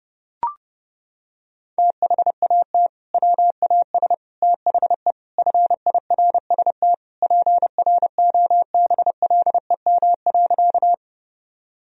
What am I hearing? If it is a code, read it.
Morse code, heard as a keyed tone, about 30 wpm, 700 Hz. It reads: THAT WAS THE FIRST PROBLEM.
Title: Sentences from Top 1000 Words - Review Part 2 - 30wpm